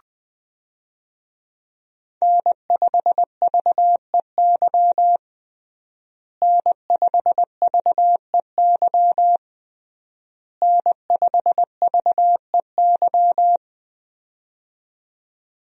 Morse code, heard as a keyed tone, about 20 wpm, 700 Hz. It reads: N5VEY N5VEY N5VEY
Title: USA Call Signs - 5 Characters - Repeated 3x - 20wpm